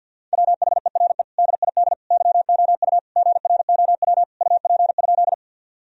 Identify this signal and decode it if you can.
Morse code, heard as a keyed tone, about 52 wpm, 700 Hz. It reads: WHERE DID YOU GROW UP?
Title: Sentences from Top 1000 Words - Review Part 1 - 50wpm